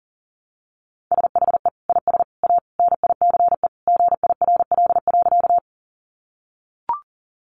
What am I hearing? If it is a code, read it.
Morse code, heard as a keyed tone, about 40 wpm, 700 Hz. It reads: SHE IS A NICE GIRL.
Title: Sentences from Top 1100 Words 40wpm